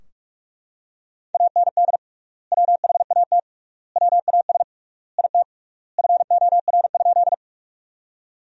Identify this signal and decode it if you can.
Morse code, heard as a keyed tone, about 42 wpm, 700 Hz. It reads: AND WHAT WAS IT FOR?
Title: 3x Word-Spacing - Sentences from Top 100 Words - 45wpm